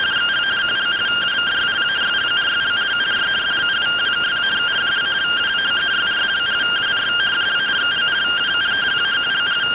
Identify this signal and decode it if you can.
Two-tone teleprinter signal, about 170 Hz shift.